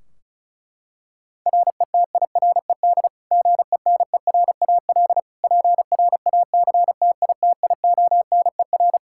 Morse code, heard as a keyed tone, about 34 wpm, 700 Hz. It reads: RETIRED GENERAL PRACTITIONER
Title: Practice QSO - Exchange 3 - More Info - 35wpm